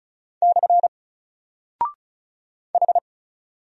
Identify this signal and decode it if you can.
Morse code, heard as a keyed tone, about 35 wpm, 700 Hz. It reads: / H
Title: Single Character (Letter:Number:Punctuation:Prosign) 35wpm